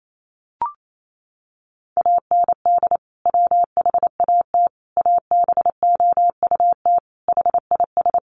Morse code, heard as a keyed tone, about 28 wpm, 700 Hz. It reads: AND WHAT ABOUT HIS
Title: Sentences from Top 1200 Words 28wpm